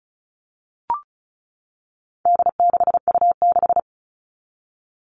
Morse code, heard as a keyed tone, about 34 wpm, 700 Hz. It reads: D6U6
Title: All Cut Numbers - 4-Digit 35wpm